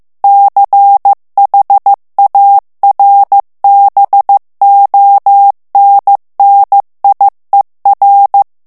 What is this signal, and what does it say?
Morse code, heard as a keyed tone, about 15 wpm, 800 Hz. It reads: CHARBONNIER
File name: morse mp3 / FRANCAIS TEXTE CLAIR 1 VITESSE 14 MOTS MINUTES